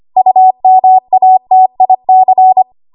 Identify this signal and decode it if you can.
Morse code, heard as a keyed tone, about 25 wpm, 750 Hz. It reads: UMATIC
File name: Archive / 25 WPM / 25WPM